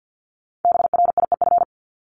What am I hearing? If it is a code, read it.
Morse code, heard as a keyed tone, about 50 wpm, 700 Hz. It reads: BRIEF
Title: Multi-letter Phonemes 50wpm